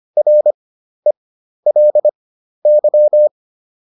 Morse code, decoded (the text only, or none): RELY